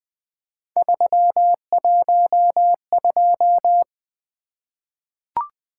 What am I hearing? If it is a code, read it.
Morse code, heard as a keyed tone, about 20 wpm, 700 Hz. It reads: 312
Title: Three Letter-Number Combinations - 20wpm